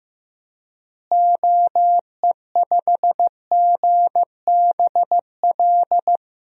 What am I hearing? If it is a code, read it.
Morse code, heard as a keyed tone, about 15 wpm, 700 Hz. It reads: OE5GBL
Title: DXCC-100 Call Signs - 6 Characters - Repeated 2x - 15wpm